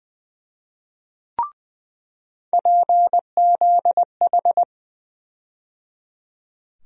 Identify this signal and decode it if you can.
Morse code, heard as a keyed tone, about 20 wpm, 700 Hz. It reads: PZH
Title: USA Call Sign Postfixes - 3 Letters 20wpm